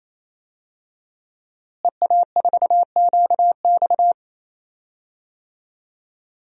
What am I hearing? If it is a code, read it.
Morse code, heard as a keyed tone, about 28 wpm, 700 Hz. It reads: EA4QX
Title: DXCC-100 Call Signs - 5 Characters - Repeated 3x - 28wpm